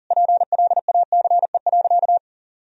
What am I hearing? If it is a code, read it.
Morse code, heard as a keyed tone, about 37 wpm, 700 Hz. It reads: PLACE.
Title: Sentences from Top 1000 Words - Review Part 3 - 40wpm